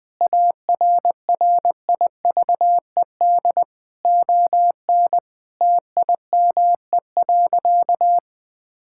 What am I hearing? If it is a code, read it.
Morse code, heard as a keyed tone, about 20 wpm, 700 Hz. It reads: ARRIVED ON TIME.
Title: Sentences with 4-Word Phrases - 20wpm